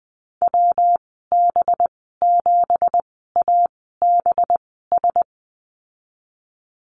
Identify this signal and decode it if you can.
Morse code, heard as a keyed tone, about 20 wpm, spaced as for 15 wpm, 700 Hz. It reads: WB7ABS
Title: 207 Lesson 33g - Letter j - Call Signs 20wpm